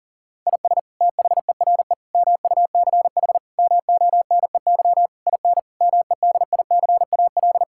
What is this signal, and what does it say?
Morse code, heard as a keyed tone, about 40 wpm, 700 Hz. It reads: IS THERE MUCH MONEY IN MEDICAL
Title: Sentences from Top 1000 Words - Review Part 5 - 40wpm